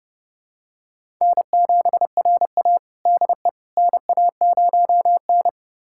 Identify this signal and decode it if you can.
Morse code, heard as a keyed tone, about 29 wpm, 700 Hz. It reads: N7RA DE NA0N
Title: USA Callsign DE Callsign - 4 Characters - 30wpm